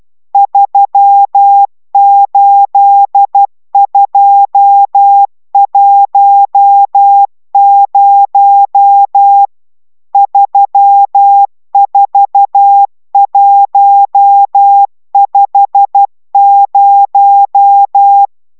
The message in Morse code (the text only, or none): 38210 34150